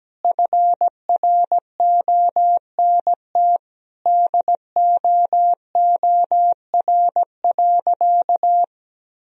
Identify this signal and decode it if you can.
Morse code, heard as a keyed tone, about 17 wpm, 700 Hz. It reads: FRONT DOOR.
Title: Sentences from Top 1000 Words - Review Part 2 - 17wpm